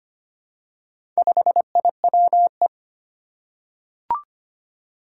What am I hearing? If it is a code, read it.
Morse code, heard as a keyed tone, about 25 wpm, 700 Hz. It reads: 5IWE
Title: Four Challenging Letter-Number Combinations 25wpm